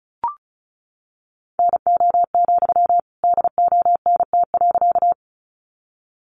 Morse code, heard as a keyed tone, about 34 wpm, 700 Hz. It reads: NO, DONT.
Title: Sentences from Top 300 Words 35wpm